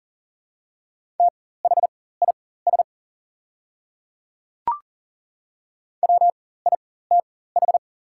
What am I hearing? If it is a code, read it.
Morse code, heard as a keyed tone, about 40 wpm, spaced as for 19 wpm, 700 Hz. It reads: THIS WITH
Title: 4x Character Spacing - Top 100 Words - 40wpm